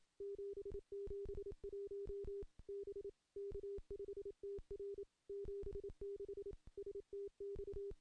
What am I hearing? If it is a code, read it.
Morse code, heard as a keyed tone, about 28 wpm, 400 Hz. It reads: 771 B K5TR 76 STX
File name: contests / CW-SS-k5tr- -0700